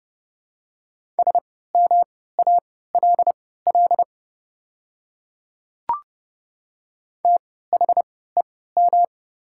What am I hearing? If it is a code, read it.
Morse code, heard as a keyed tone, about 31 wpm, spaced as for 17 wpm, 690 Hz. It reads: SMALL THEM
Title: 3x Character Spacing - Top 200 Words - 30wpm